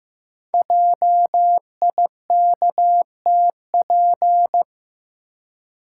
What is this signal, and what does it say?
Morse code, heard as a keyed tone, about 15 wpm, 700 Hz. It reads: JIKTP
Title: Code Groups - 5 Letters - 15wpm